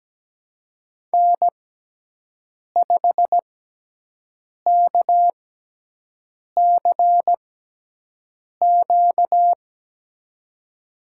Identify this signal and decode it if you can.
Morse code, heard as a keyed tone, about 17 wpm, spaced as for 6 wpm, 700 Hz. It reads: N5KCQ
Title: USA Call Signs - 6x Character Spacing - 17wpm